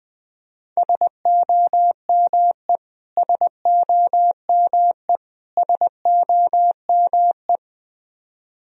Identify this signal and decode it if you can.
Morse code, heard as a keyed tone, about 21 wpm, 700 Hz. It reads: SOME SOME SOME
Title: Top 100 Words Repeated 20wpm